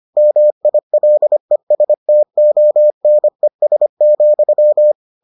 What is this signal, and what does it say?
Morse code, heard as a keyed tone, about 25 wpm, 600 Hz. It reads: MILESTONES,